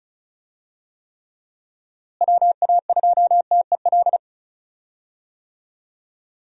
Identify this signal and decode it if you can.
Morse code, heard as a keyed tone, about 34 wpm, 700 Hz. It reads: WA2TEL
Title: USA Call Signs - 6 Characters - Repeated 2x - 35wpm